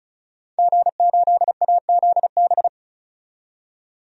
Morse code, heard as a keyed tone, about 34 wpm, 700 Hz. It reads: G8AZB
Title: DXCC-100 Call Signs - 5 Characters - 35wpm